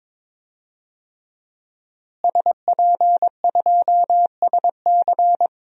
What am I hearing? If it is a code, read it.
Morse code, heard as a keyed tone, about 22 wpm, 700 Hz. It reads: SP2SC
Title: DXCC-100 Call Signs - 5 Characters - Repeated 3x - 22wpm